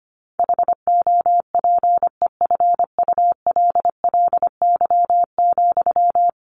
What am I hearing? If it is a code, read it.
Morse code, heard as a keyed tone, about 25 wpm, 700 Hz. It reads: HOPEFULLY,
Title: Sentences from Top 1500 Words 25wpm